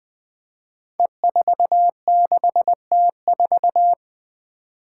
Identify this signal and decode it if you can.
Morse code, heard as a keyed tone, about 20 wpm, 700 Hz. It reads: E46T4
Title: Five Challenging Letter-Number Combinations 20wpm